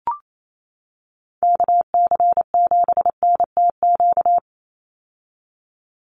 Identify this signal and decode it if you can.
Morse code, heard as a keyed tone, about 28 wpm, 700 Hz. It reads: KC7NTQ
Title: USA Call Signs 28wpm